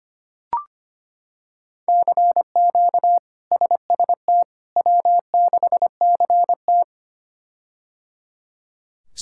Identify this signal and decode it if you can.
Morse code, heard as a keyed tone, about 25 wpm, 700 Hz. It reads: CQ SST W6CT
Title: K1USN SST QSO Practice - 25wpm